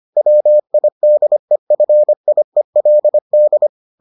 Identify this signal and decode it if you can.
Morse code, heard as a keyed tone, about 25 wpm, 600 Hz. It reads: WIDEFIELD